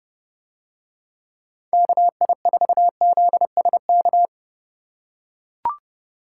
Morse code, heard as a keyed tone, about 30 wpm, 700 Hz. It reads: KI4ZSK